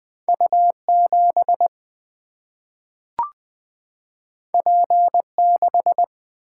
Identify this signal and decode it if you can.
Morse code, heard as a keyed tone, about 20 wpm, 700 Hz. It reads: U7 P6
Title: Single-Letter Single-Number 20wpm